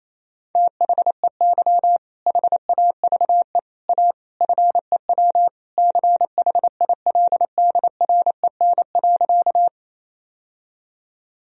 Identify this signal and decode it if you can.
Morse code, heard as a keyed tone, about 28 wpm, 700 Hz. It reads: THEY HAVE A FEW CHILDREN.
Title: Sentences from Top 1000 Words - Review Part 1 - 28wpm